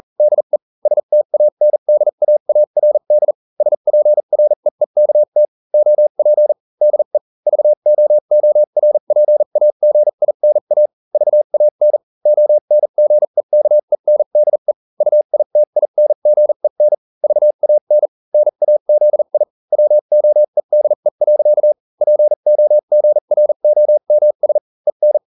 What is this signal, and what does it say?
Morse code, heard as a keyed tone, about 40 wpm, spaced as for 35 wpm, 600 Hz. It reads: DE STANDAARD SPREEKT OP DE VOORPAGINA VAN ONGEKENDE UITINGEN VAN NAZI WOEDE. POGROMS EN